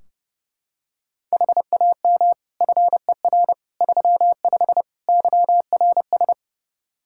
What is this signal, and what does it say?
Morse code, heard as a keyed tone, about 30 wpm, 700 Hz. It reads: HAM FER 35 YRS
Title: Practice QSO - Exchange 3 - More Info - 30wpm